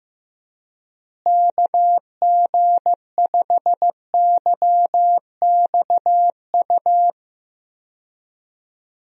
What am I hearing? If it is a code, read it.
Morse code, heard as a keyed tone, about 15 wpm, 700 Hz. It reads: KG5YXU
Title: USA Call Signs - 6 Characters - Repeated 3x - 15wpm